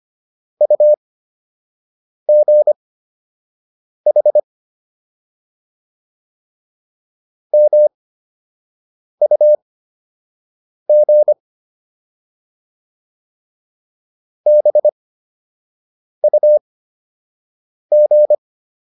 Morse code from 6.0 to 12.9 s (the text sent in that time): MUG